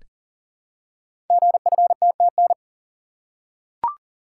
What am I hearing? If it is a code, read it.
Morse code, heard as a keyed tone, about 40 wpm, 700 Hz. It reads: GFTTN
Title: Code Groups - 5 Letters - 40wpm